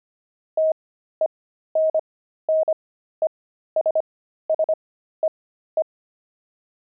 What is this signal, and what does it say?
Morse code, heard as a keyed tone, about 25 wpm, spaced as for 13 wpm, 630 Hz. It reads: TENNESSEE
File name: pangram-fs-13-cs-25-sissy